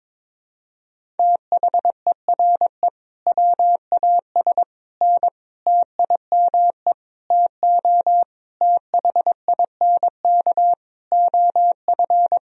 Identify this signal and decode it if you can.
Morse code, heard as a keyed tone, about 22 wpm, 700 Hz. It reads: THERE WAS N TIME TO THINK OF
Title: Sentences from Top 400 Words and Abbreviations 22wpm